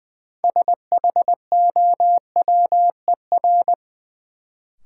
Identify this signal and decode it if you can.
Morse code, heard as a keyed tone, about 21 wpm, 700 Hz. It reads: SHOWER
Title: Sentences from Top 1600 Words - New Word Drill - 20wpm